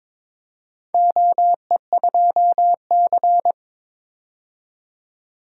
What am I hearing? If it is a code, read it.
Morse code, heard as a keyed tone, about 22 wpm, 700 Hz. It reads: OE2C
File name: practice / 22wpm / Four Letter-Number Combinations - 22wpm